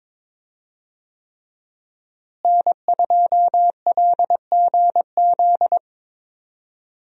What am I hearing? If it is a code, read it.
Morse code, heard as a keyed tone, about 21 wpm, 700 Hz. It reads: N2LGZ
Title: USA Call Signs - 5 Characters - Repeated 3x - 22wpm